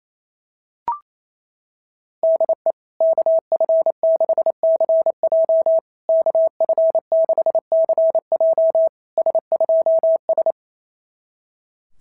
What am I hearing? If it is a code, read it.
Morse code, heard as a keyed tone, about 28 wpm, 660 Hz. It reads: DE KF6CJ KF6CJ S2S
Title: SOTA QSO Practice 28wpm